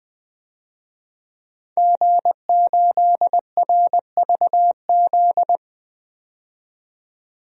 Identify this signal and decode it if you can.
Morse code, heard as a keyed tone, about 21 wpm, 700 Hz. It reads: G8RVZ
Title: DXCC-100 Call Signs - 5 Characters - 20wpm